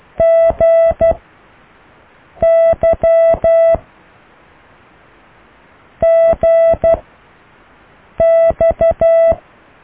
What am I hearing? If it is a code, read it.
Morse code, heard as a keyed tone, about 12 wpm, 660 Hz. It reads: G Y G X